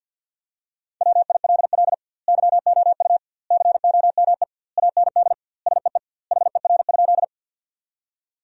Morse code, heard as a keyed tone, about 50 wpm, 700 Hz. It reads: WILL YOU COME AND SEE HER?
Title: 2x Word-Spacing - Sentences from Top 100 Words - 50wpm